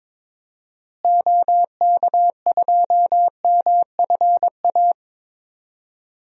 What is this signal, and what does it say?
Morse code, heard as a keyed tone, about 22 wpm, 700 Hz. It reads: OK2MFA
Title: DXCC-100 Call Signs - 6 Characters - Repeated 3x - 22wpm